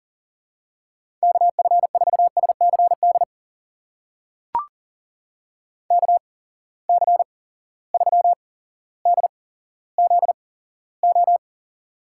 Morse code, heard as a keyed tone, about 41 wpm, spaced as for 17 wpm, 700 Hz. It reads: KF4SCD KC3DZO
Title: USA Call Signs - 6 Characters - 8x Character Spacing - 40wpm